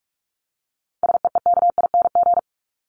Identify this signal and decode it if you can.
Morse code, heard as a keyed tone, about 44 wpm, 700 Hz. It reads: SEEKING
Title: ING Suffix - Top 100 Words with Base Words 45wpm